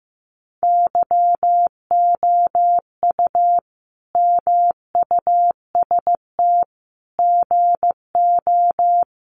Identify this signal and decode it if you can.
Morse code, heard as a keyed tone, about 15 wpm, 700 Hz. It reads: YOU MUST GO
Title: Sentences from Top 1100 Words 15wpm